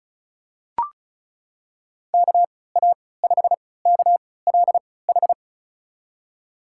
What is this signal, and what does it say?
Morse code, heard as a keyed tone, about 36 wpm, spaced as for 20 wpm, 700 Hz. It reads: KA5KLH
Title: USA Call Signs - 6 Characters - 3x Character Spacing - 35wpm